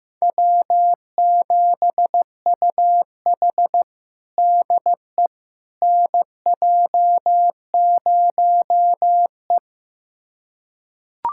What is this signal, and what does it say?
Morse code, heard as a keyed tone, about 16 wpm, 700 Hz. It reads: W7UH DE NJ0E
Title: USA Callsign DE Callsign - 4 Characters - 15wpm